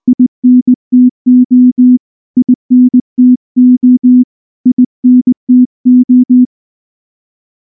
Morse code, heard as a keyed tone, about 19 wpm, 250 Hz. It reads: INTO INTO INTO
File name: morse / morse-training-20WPM-7